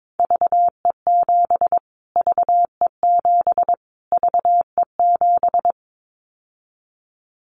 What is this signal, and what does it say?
Morse code, heard as a keyed tone, about 22 wpm, 700 Hz. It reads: VE7 VE7 VE7
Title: Callsign Prefixes for Top 25 DXCC - Repeated 3x - 22wpm